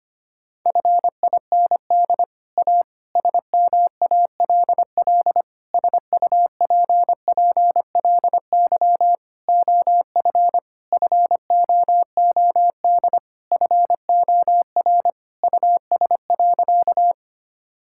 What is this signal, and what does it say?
Morse code, heard as a keyed tone, about 25 wpm, 700 Hz. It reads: FIND A SMALL SUPPLY OF FOOD FOR US.
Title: Sentences from Top 1000 Words - Review Part 5 - 25wpm